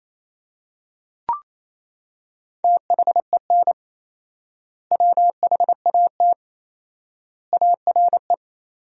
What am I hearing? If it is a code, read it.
Morse code, heard as a keyed tone, about 28 wpm, 700 Hz. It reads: THEN WHAT ARE
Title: 4x Word-Spacing - Sentences from Top 100 Words - 28wpm